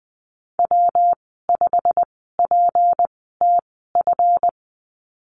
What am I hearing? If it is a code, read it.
Morse code, heard as a keyed tone, about 20 wpm, spaced as for 15 wpm, 700 Hz. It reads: W5PTF
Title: 193 Lesson 31g - Letter v - Call Signs 20wpm